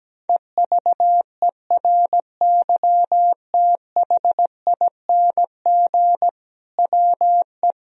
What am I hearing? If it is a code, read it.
Morse code, heard as a keyed tone, about 17 wpm, 700 Hz. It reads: EVERYTHING WE
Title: Sentences from Top 1000 Words - Review Part 1 - 17wpm